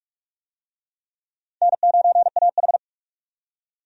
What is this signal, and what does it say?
Morse code, heard as a keyed tone, about 44 wpm, 700 Hz. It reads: N9AH
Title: USA Call Signs - 4 Characters - 45wpm